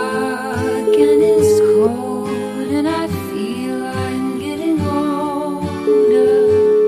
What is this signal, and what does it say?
Ringback tone (Europe, 425 Hz), with noise.